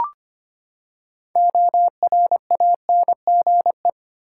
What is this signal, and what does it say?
Morse code, heard as a keyed tone, about 25 wpm, 700 Hz. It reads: ORANGE
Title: Six Letter Words - 25wpm